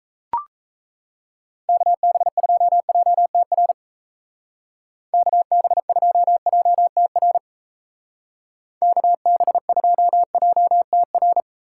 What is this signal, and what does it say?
Morse code, heard as a keyed tone, about 34 wpm, 700 Hz. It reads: KB2JTR KB2JTR KB2JTR
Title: Speed Racing - USA Call Signs 28wpm